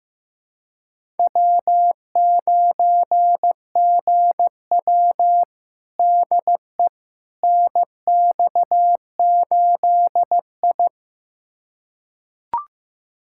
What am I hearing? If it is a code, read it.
Morse code, heard as a keyed tone, about 15 wpm, 700 Hz. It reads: W9GW DE NX8I
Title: USA Callsign DE Callsign - 4 Characters - 15wpm